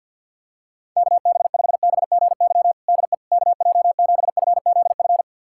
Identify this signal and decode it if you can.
Morse code, heard as a keyed tone, about 50 wpm, 700 Hz. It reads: KB5BGY DE KJ7FZF
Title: USA Callsign DE Callsign 50wpm